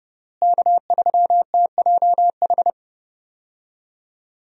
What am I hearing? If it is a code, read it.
Morse code, heard as a keyed tone, about 30 wpm, 700 Hz. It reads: K3TJH
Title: USA Call Signs - 5 Characters - Repeated 3x - 30wpm